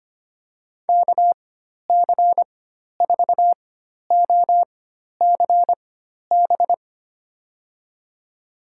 Morse code, heard as a keyed tone, about 25 wpm, spaced as for 12 wpm, 700 Hz. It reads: KC4OCB
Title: USA Call Signs - 6 Characters - 4x Character Spacing - 25wpm